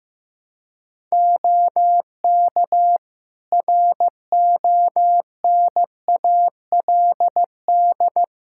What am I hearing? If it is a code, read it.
Morse code, heard as a keyed tone, about 15 wpm, 700 Hz. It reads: OK RONALD
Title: Practice QSO - Exchange 4 - Ending - 15wpm